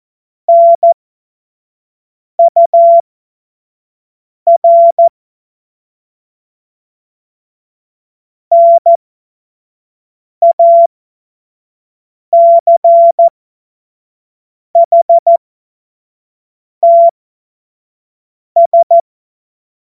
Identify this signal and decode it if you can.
Morse code, heard as a keyed tone, about 14 wpm, spaced as for 5 wpm, 680 Hz. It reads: NUR NACHTS